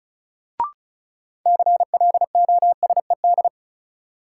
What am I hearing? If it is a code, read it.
Morse code, heard as a keyed tone, about 34 wpm, 700 Hz. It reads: CLOSED
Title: Sentences from Top 1400 Words - New Word Drill Alt - 35wpm